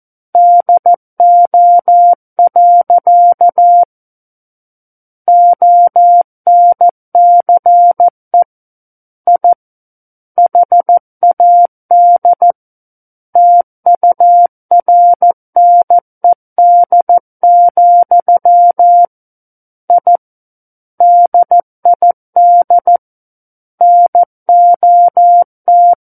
Morse code, heard as a keyed tone, about 14 wpm, 700 Hz. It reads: DO. ONCE I HAD TURNED, I DID NOT
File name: Book One / 05-Chapter 5 - 14 WPM